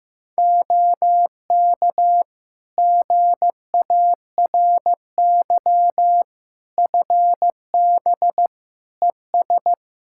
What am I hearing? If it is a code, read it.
Morse code, heard as a keyed tone, about 15 wpm, 700 Hz. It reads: OK GARY FB ES
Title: Practice QSO - Exchange 2 - Info - 15wpm